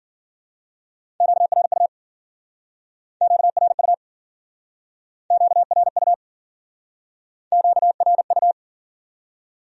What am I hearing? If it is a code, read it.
Morse code, heard as a keyed tone, about 49 wpm, 700 Hz. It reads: QRU QRU QRU QRU
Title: Speed-Racing Q-Codes - 40wpm